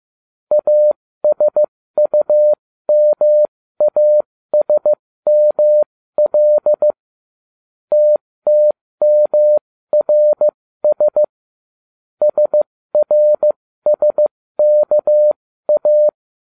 Morse code, heard as a keyed tone, about 15 wpm, spaced as for 13 wpm, 600 Hz. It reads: ASUMASML TTMRS SRSKA